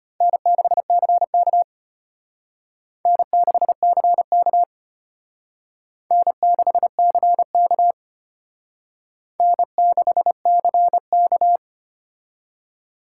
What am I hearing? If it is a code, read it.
Morse code, heard as a keyed tone, about 32 wpm, 700 Hz. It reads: N6CK N6CK N6CK N6CK